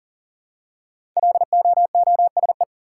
Morse code, heard as a keyed tone, about 40 wpm, 700 Hz. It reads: LOOSE
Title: Five Letter Words - 40wpm